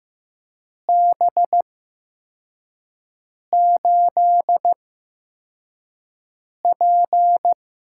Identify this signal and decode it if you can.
Morse code, heard as a keyed tone, about 15 wpm, spaced as for 4 wpm, 700 Hz. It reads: B8P
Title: USA Call Signs - 4 Characters - 8x Character Spacing - 15wpm